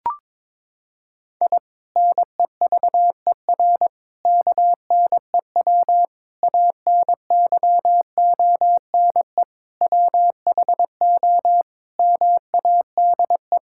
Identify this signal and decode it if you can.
Morse code, heard as a keyed tone, about 22 wpm, 700 Hz. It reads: I NEVER KNEW ANYONE WHO MADE
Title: Sentences from Top 1000 Words - Review Part 6 - 22wpm